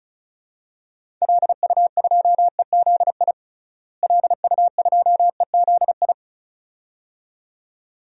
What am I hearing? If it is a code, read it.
Morse code, heard as a keyed tone, about 34 wpm, 700 Hz. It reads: LU2EZI LU2EZI
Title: DXCC-100 Call Signs - 6 Characters - Repeated 2x - 35wpm